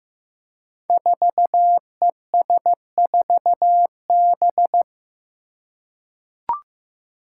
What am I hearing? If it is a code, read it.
Morse code, heard as a keyed tone, about 15 wpm, 700 Hz. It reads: 4ES4B
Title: Five Challenging Letter-Number Combinations 15wpm